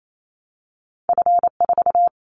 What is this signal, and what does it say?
Morse code, heard as a keyed tone, about 28 wpm, 700 Hz. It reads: F4
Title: Callsign Prefixes for Top 50 DXCC - 28wpm